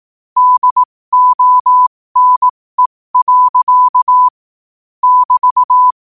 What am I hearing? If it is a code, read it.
Morse code, heard as a keyed tone, about 18 wpm, spaced as for 15 wpm, 1000 Hz. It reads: DONE. <BT>
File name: qotdfiles / 15wpm